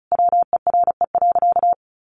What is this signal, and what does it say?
Morse code, heard as a keyed tone, about 34 wpm, 700 Hz. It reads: WERE.
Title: Sentences from Top 900 Words 35wpm